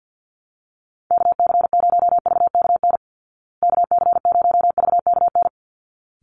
Morse code, heard as a keyed tone, about 49 wpm, 700 Hz. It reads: KC0VKN KC0VKN
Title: Call Signs Repeated 3x - 50wpm